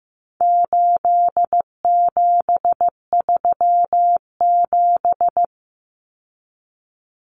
Morse code, heard as a keyed tone, about 15 wpm, 700 Hz. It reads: 8737
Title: Common Cut Numbers - 4-Digit 15wpm